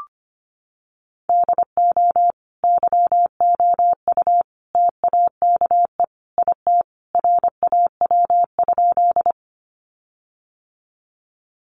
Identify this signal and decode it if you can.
Morse code, heard as a keyed tone, about 25 wpm, 700 Hz. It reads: DO YOU TAKE IT RAW?
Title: Sentences from Top 1500 Words 25wpm